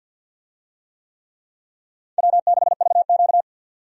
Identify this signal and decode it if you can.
Morse code, heard as a keyed tone, about 52 wpm, 700 Hz. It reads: W6VQ
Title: USA Call Signs - 4 Characters - 50wpm